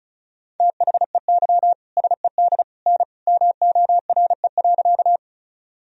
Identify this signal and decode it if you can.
Morse code, heard as a keyed tone, about 36 wpm, 700 Hz. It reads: THEY SED N MORE.
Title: Sentences from Top 100 Words and Abbreviations 35wpm